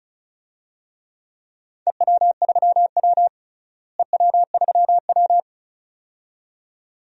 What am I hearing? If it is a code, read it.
Morse code, heard as a keyed tone, about 34 wpm, 700 Hz. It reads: EW3W EW3W
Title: DXCC-100 Call Signs - 4 Characters - Repeated 2x - 35wpm